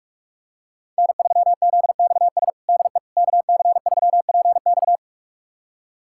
Morse code, heard as a keyed tone, about 42 wpm, 700 Hz. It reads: N3ZXS DE KC3PX
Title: USA Callsign DE Callsign - 5 Characters - 45wpm